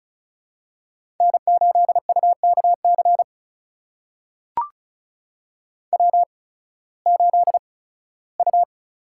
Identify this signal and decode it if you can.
Morse code, heard as a keyed tone, about 36 wpm, spaced as for 19 wpm, 700 Hz. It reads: N8UKC W8U
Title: USA Call Signs - 5 Characters - 8x Character Spacing - 35wpm